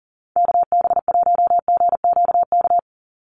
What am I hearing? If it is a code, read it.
Morse code, heard as a keyed tone, about 38 wpm, 690 Hz. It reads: KB1GQK
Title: USA Call Signs 40wpm